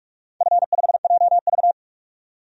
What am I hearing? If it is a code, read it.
Morse code, heard as a keyed tone, about 46 wpm, 700 Hz. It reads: F5JV
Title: DXCC-100 Call Signs - 4 Characters - 45wpm